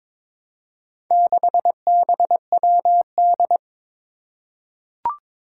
Morse code, heard as a keyed tone, about 22 wpm, 700 Hz. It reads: 6BWD
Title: Four Challenging Letter-Number Combinations 22wpm